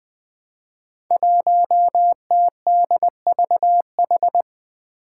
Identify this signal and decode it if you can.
Morse code, heard as a keyed tone, about 20 wpm, 700 Hz. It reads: 1TDVH
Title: Five Challenging Letter-Number Combinations 20wpm